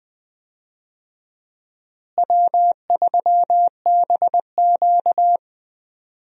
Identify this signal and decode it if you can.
Morse code, heard as a keyed tone, about 20 wpm, 700 Hz. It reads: W3BQ
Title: USA Call Signs - 4 Characters - Repeated 2x - 20wpm